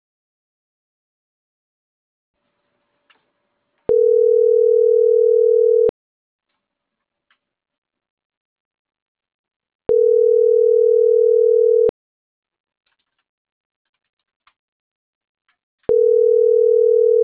Ringback tone (North America), 440 + 480 Hz.